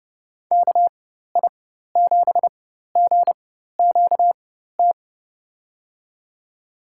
Morse code, heard as a keyed tone, about 31 wpm, spaced as for 14 wpm, 700 Hz. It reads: KI7GQT
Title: USA Call Signs - 6 Characters - 4x Character Spacing - 30wpm